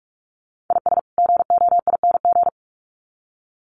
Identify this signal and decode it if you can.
Morse code, heard as a keyed tone, about 44 wpm, 700 Hz. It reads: IS GOING
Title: Sets of 2 Words 45wpm